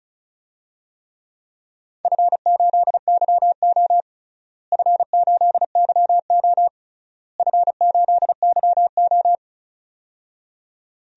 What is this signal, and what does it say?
Morse code, heard as a keyed tone, about 36 wpm, 700 Hz. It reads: F8YO F8YO F8YO
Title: DXCC-100 Call Signs - 4 Characters - Repeated 3x - 35wpm